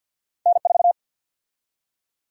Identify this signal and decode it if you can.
Morse code, heard as a keyed tone, about 53 wpm, 700 Hz. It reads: N4
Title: USA Call Sign Prefixes - 50wpm